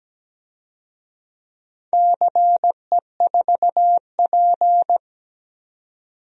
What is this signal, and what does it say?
Morse code, heard as a keyed tone, about 17 wpm, 700 Hz. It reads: CE4P
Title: DXCC-100 Call Signs - 4 Characters - 17wpm